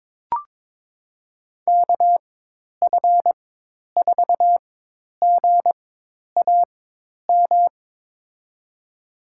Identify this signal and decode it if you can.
Morse code, heard as a keyed tone, about 23 wpm, spaced as for 10 wpm, 700 Hz. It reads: KF4GAM
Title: USA Call Signs - 6 Characters - 4x Character Spacing - 22wpm